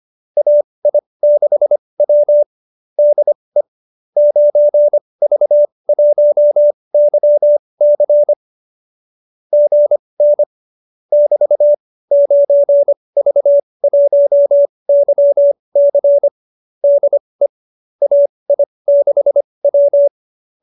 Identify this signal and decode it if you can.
Morse code, heard as a keyed tone, about 25 wpm, spaced as for 20 wpm, 590 Hz. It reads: AI6W DE 9V1YC GN <BT> 9V1YC DE AI6W
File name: qso207b 20